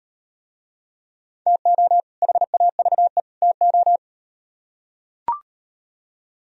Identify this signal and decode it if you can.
Morse code, heard as a keyed tone, about 37 wpm, 700 Hz. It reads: TO HAVE TO